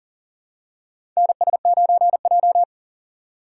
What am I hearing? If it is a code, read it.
Morse code, heard as a keyed tone, about 38 wpm, 700 Hz. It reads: NS9J
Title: USA Call Signs - 4 Characters - 3x Character Spacing - 40wpm